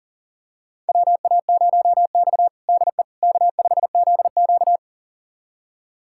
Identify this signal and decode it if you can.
Morse code, heard as a keyed tone, about 40 wpm, 700 Hz. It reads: WA0X DE K5ZQ
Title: USA Callsign DE Callsign - 4 Characters - 40wpm